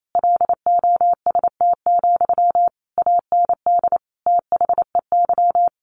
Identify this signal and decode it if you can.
Morse code, heard as a keyed tone, about 28 wpm, 700 Hz. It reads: LOST, AND THEY
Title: A Sound Of Thunder 28wpm